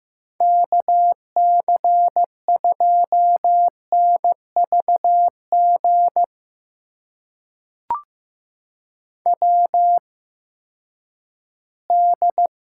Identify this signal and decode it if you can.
Morse code, heard as a keyed tone, about 15 wpm, spaced as for 10 wpm, 700 Hz. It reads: KC2NVG WD
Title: USA Call Signs - 6 Characters - 8x Character Spacing - 15wpm